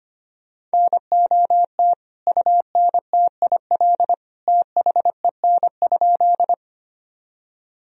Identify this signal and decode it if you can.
Morse code, heard as a keyed tone, about 25 wpm, 700 Hz. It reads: NOT UNTIL THEN?
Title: Sentences from Top 1000 Words - Review Part 1 - 25wpm